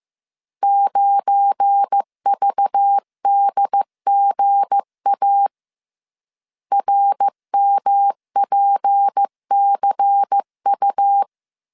Morse code, heard as a keyed tone, about 15 wpm, 800 Hz. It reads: 9VDGA RMPCU
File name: CW / DET / 1 9 1500C S A 14wpm 3222